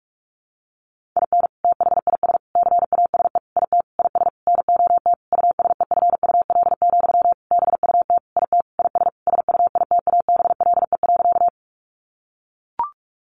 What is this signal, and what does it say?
Morse code, heard as a keyed tone, about 44 wpm, 700 Hz. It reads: IN THIS CASE IT IS NOT USEFUL, BUT IT IS SUITABLE.
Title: Sentences from Top 1400 Words 45wpm